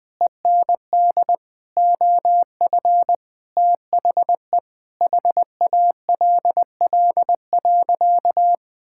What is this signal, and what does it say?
Morse code, heard as a keyed tone, about 20 wpm, 700 Hz. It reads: END OF THE HALL.
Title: Sentences with 5-Word Phrases - 20wpm